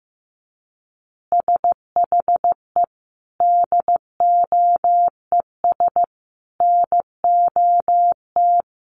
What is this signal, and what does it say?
Morse code, heard as a keyed tone, about 15 wpm, 700 Hz. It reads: SHE DOES NOT
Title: Easy Phrases 15wpm